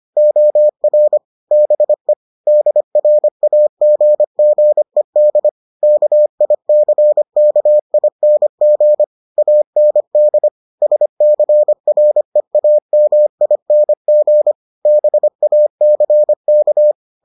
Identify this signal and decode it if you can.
Morse code, heard as a keyed tone, about 25 wpm, 600 Hz. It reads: OR BE DRAGGED KICKING AND SCREAMING BACK